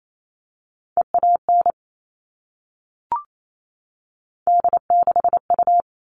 Morse code, heard as a keyed tone, about 28 wpm, 700 Hz. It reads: EAN D6U
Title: All Cut Numbers - 3-Digit 28wpm